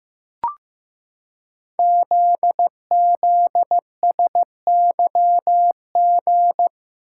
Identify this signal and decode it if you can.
Morse code, heard as a keyed tone, about 15 wpm, 700 Hz. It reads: ZZSYG
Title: Code Groups - 5 Letters - 15wpm